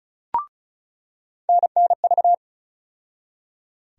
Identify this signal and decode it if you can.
Morse code, heard as a keyed tone, about 36 wpm, 700 Hz. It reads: NNV
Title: USA Call Sign Postfixes - 3 Letters 35wpm